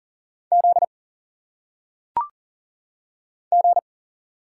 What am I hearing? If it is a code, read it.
Morse code, heard as a keyed tone, about 39 wpm, 700 Hz. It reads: Z G
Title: Single Letters - Phonetic Pronunciation 40wpm